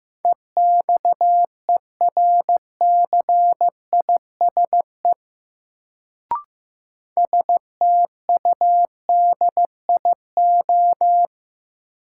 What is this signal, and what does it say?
Morse code, heard as a keyed tone, about 15 wpm, 700 Hz. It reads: EXERCISE STUDIO
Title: Sentences from Top 1200 Words - New Word Drill Alt - 15wpm